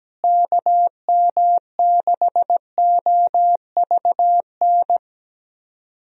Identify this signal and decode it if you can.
Morse code, heard as a keyed tone, about 17 wpm, 700 Hz. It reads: KM6OVN
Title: USA Call Signs - 6 Characters - Repeated 2x - 17wpm